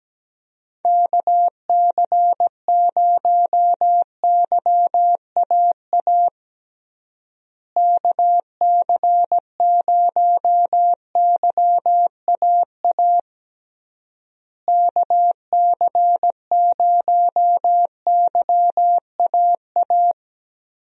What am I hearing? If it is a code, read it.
Morse code, heard as a keyed tone, about 17 wpm, 700 Hz. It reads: KC0YAA KC0YAA KC0YAA
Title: USA Call Signs - 6 Characters - Repeated 3x - 17wpm